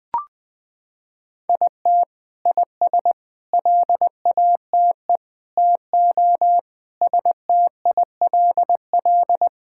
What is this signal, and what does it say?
Morse code, heard as a keyed tone, about 20 wpm, 700 Hz. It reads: IT IS LATE TO STILL
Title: Sentences from Top 1000 Words - Review Part 5 - 20wpm